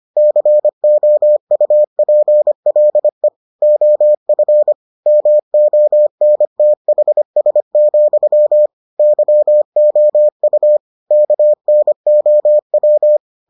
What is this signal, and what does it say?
Morse code, heard as a keyed tone, about 25 wpm, 600 Hz. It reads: COUPLE OF MONTHS, YOU KNOW